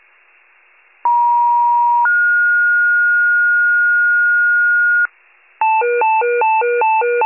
Two-tone page, heard as a one-second tone, then a three-second tone, about 950 then 1430 Hz, with noise. It is heard from 1.0 to 5.1 s.